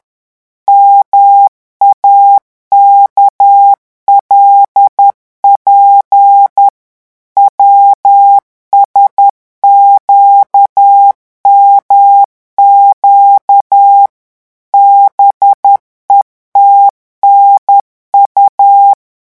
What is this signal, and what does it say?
Morse code, heard as a keyed tone, about 10 wpm, 800 Hz. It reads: MAKLP WSQMQ BETNU
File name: morse mp3 / revision8-vitesse 10 mots